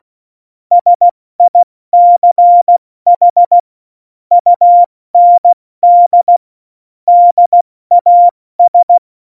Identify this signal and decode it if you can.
Morse code, heard as a keyed tone, about 16 wpm, 700 Hz. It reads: SICH UND DAS
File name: Morsekurs / Witze / Witz 08. Dezember 2020, 70 F80